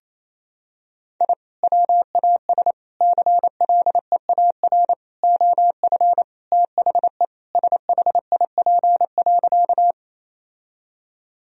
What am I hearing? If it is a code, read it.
Morse code, heard as a keyed tone, about 28 wpm, 700 Hz. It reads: I WAS CLEAR OF THE SHIP.
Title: Sentences from Top 1000 Words - Review Part 2 - 28wpm